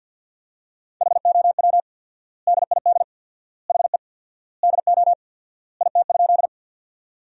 Morse code, heard as a keyed tone, about 49 wpm, 700 Hz. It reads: HOW DID HE DO IT?
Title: 4x Word-Spacing - Sentences from Top 100 Words - 50wpm